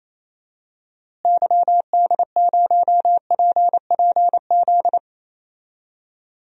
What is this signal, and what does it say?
Morse code, heard as a keyed tone, about 28 wpm, 700 Hz. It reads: YD0PPZ
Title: DXCC-100 Call Signs - 6 Characters - Repeated 3x - 28wpm